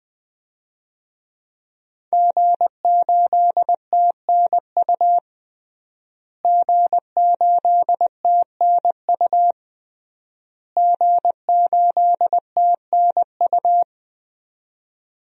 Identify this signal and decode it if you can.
Morse code, heard as a keyed tone, about 21 wpm, 700 Hz. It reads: G8TNU G8TNU G8TNU
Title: DXCC-100 Call Signs - 5 Characters - Repeated 3x - 20wpm